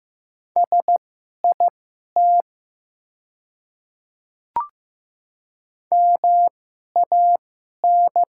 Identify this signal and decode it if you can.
Morse code, heard as a keyed tone, about 15 wpm, spaced as for 11 wpm, 700 Hz. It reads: SIT MAN
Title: 2x Character Spacing - Top 500 Words - 15wpm